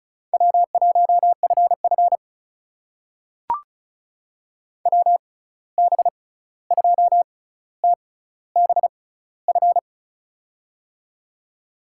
Morse code, heard as a keyed tone, about 36 wpm, spaced as for 16 wpm, 700 Hz. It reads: W1FF WB2TBF
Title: USA Call Signs - 6x Character Spacing - 35wpm